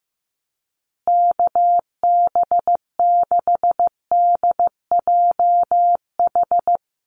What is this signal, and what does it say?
Morse code, heard as a keyed tone, about 15 wpm, 700 Hz. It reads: KB6DJH
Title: USA Call Signs Repeated 3x 15wpm